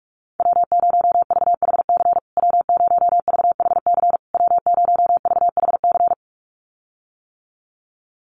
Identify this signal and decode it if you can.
Morse code, heard as a keyed tone, about 44 wpm, 700 Hz. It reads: W0VHC W0VHC W0VHC
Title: USA Call Signs Repeated 3x 45wpm